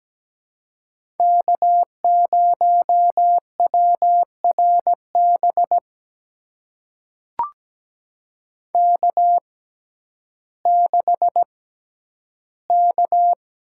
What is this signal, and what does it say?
Morse code, heard as a keyed tone, about 17 wpm, spaced as for 10 wpm, 700 Hz. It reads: K0WRB K6K
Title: USA Call Signs - 5 Characters - 6x Character Spacing - 17wpm